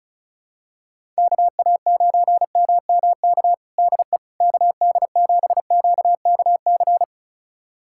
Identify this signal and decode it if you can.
Morse code, heard as a keyed tone, about 36 wpm, 700 Hz. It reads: KA9MMK DE KD7QKC
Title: USA Callsign DE Callsign 35wpm